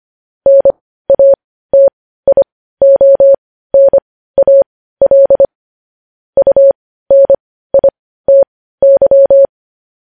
Morse code, heard as a keyed tone, about 25 wpm, spaced as for 15 wpm, 550 Hz. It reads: NATIONAL UNITY
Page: News headlines at 15